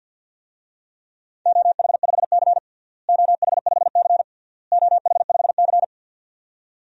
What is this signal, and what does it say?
Morse code, heard as a keyed tone, about 51 wpm, 700 Hz. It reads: OH5C OH5C OH5C
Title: DXCC-100 Call Signs - 4 Characters - Repeated 3x - 50wpm